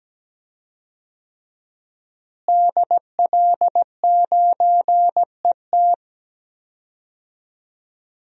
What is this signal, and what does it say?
Morse code, heard as a keyed tone, about 17 wpm, 700 Hz. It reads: DL9ET
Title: DXCC-100 Call Signs - 5 Characters - Repeated 3x - 17wpm